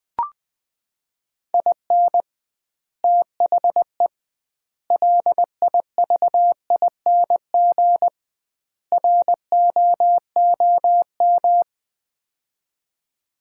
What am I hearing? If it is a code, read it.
Morse code, heard as a keyed tone, about 20 wpm, 700 Hz. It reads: IN THE LIVING ROOM
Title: 2x Word-Spacing - Sets of 4 Words - 20wpm